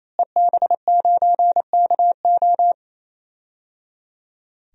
Morse code, heard as a keyed tone, about 27 wpm, 700 Hz. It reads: EB9KO
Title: DXCC-100 Call Signs - 5 Characters - Repeated 3x - 28wpm